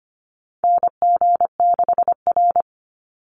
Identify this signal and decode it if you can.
Morse code, heard as a keyed tone, about 25 wpm, 700 Hz. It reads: NG6R
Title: NA SOTA Chaser Callsigns 25wpm